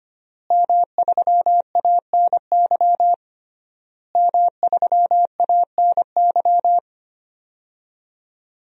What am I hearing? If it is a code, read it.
Morse code, heard as a keyed tone, about 25 wpm, 700 Hz. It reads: M3ANY M3ANY
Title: DXCC-100 Call Signs - 5 Characters - Repeated 3x - 25wpm